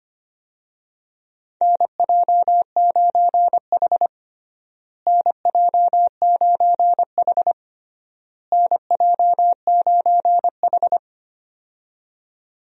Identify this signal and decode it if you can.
Morse code, heard as a keyed tone, about 25 wpm, 700 Hz. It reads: NJ9H NJ9H NJ9H
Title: USA Call Signs - 4 Characters - Repeated 3x - 25wpm